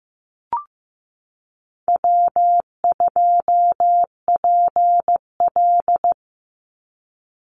Morse code, heard as a keyed tone, about 15 wpm, 700 Hz. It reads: W2PL
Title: Call Signs Repeated 3x - 15wpm